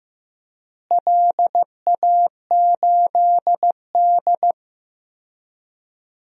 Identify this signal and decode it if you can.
Morse code, heard as a keyed tone, about 15 wpm, 700 Hz. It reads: LA8D
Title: DXCC-100 Call Signs - 4 Characters - Repeated 2x - 15wpm